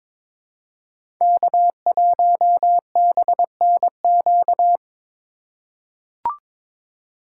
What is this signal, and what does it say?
Morse code, heard as a keyed tone, about 22 wpm, 700 Hz. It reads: K1BNQ
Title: USA Call Signs - 5 Characters - 3x Character Spacing - 22wpm